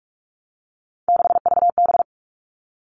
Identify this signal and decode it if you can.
Morse code, heard as a keyed tone, about 44 wpm, 700 Hz. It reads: 6VB
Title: All Cut Numbers - 3-Digit 45wpm